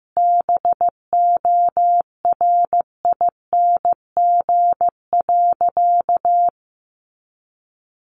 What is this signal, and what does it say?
Morse code, heard as a keyed tone, about 15 wpm, 700 Hz. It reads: BORING.
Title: Easy Phrases 15wpm